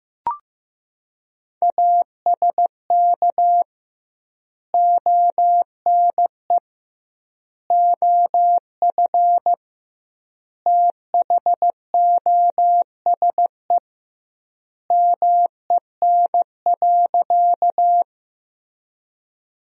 Morse code, heard as a keyed tone, about 15 wpm, 700 Hz. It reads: ASK ONE OF THOSE MEN.
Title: 2x Word-Spacing - Sentences from Top 300 Words - 15wpm